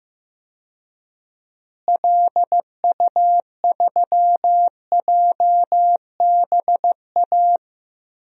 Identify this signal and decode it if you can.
Morse code, heard as a keyed tone, about 15 wpm, 700 Hz. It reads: LU3JBA
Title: DXCC-100 Call Signs - 6 Characters - Repeated 2x - 15wpm